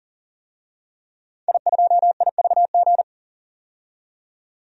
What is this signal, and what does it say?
Morse code, heard as a keyed tone, about 40 wpm, 700 Hz. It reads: I2IVG
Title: DXCC-100 Call Signs - 5 Characters - 40wpm